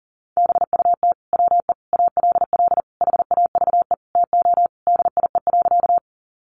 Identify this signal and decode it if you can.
Morse code, heard as a keyed tone, about 40 wpm, 700 Hz. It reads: BUT WE ALL HAVE TO DIE.
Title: Sentences from Top 1300 Words 40wpm